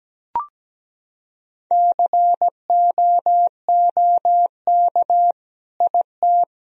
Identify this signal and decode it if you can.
Morse code, heard as a keyed tone, about 17 wpm, 700 Hz. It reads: COOK IT
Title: Sentences from Top 1000 Words - Review Part 4 - 17wpm